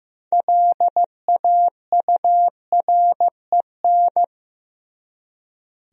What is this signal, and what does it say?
Morse code, heard as a keyed tone, about 15 wpm, 700 Hz. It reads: LAUREN
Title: USA Common Names - 15wpm